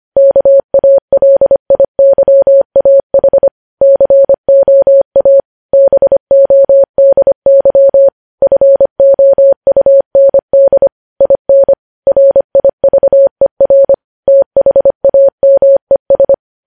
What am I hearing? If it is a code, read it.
Morse code, heard as a keyed tone, about 25 wpm, 550 Hz. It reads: KALIYAH COA BODY FOUND IN RIVER THAMES
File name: podcast / News headlines-1x-25x25